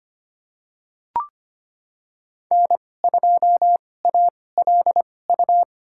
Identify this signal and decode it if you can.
Morse code, heard as a keyed tone, about 25 wpm, spaced as for 18 wpm, 700 Hz. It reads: N2ALU
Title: USA Call Signs - 5 Characters - 2x Character Spacing - 25wpm